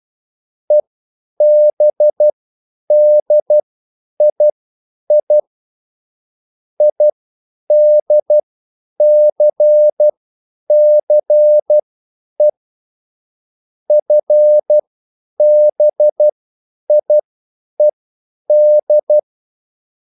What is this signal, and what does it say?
Morse code, heard as a keyed tone, about 12 wpm, spaced as for 9 wpm, 600 Hz. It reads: EBDII IDCCE FBIED
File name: QSL / CW lesson